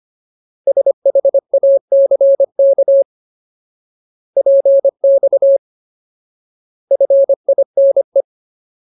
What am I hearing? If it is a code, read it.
Morse code, heard as a keyed tone, about 25 wpm, 550 Hz. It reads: SHACK PX FINE
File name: Ultimate Word List / 25WPM Ultimate